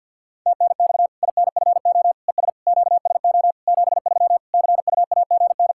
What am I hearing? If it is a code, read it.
Morse code, heard as a keyed tone, about 52 wpm, 700 Hz. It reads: TNX INFO ES QSO 73 CUAGN
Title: Practice QSO - Exchange 4 - Ending - 50wpm